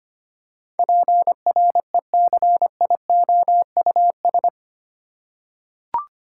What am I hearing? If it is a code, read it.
Morse code, heard as a keyed tone, about 25 wpm, 690 Hz. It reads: PRECIOUS
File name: 8 Letter Words - First 3 Letters Given - 25wpm